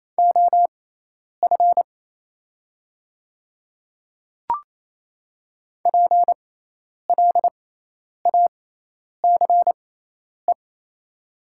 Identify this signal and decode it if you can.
Morse code, heard as a keyed tone, about 28 wpm, spaced as for 10 wpm, 700 Hz. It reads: OF PLACE
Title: 6x Character Spacing - Top 200 Words - 28wpm